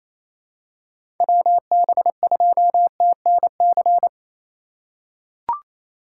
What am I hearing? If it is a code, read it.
Morse code, heard as a keyed tone, about 28 wpm, 700 Hz. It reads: WB2TNC
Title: USA Call Signs - 4x Character Spacing - 28wpm